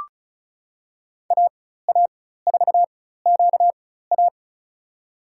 Morse code, heard as a keyed tone, about 36 wpm, spaced as for 16 wpm, 700 Hz. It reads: AA4QA
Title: USA Call Signs - 5 Characters - 4x Character Spacing - 35wpm